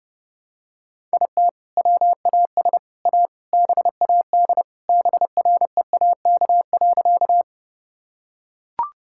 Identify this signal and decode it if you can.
Morse code, heard as a keyed tone, about 32 wpm, 700 Hz. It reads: IT WAS A BAD BREAK.
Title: Sentences from Top 1000 Words - Review Part 6 - 30wpm